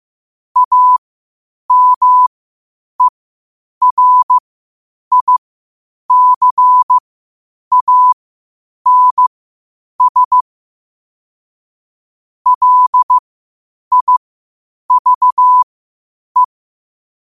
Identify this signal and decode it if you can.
Morse code, heard as a keyed tone, about 15 wpm, spaced as for 8 wpm, 1000 Hz. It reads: AMERICANS LIVE
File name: qotdfiles / 8wpm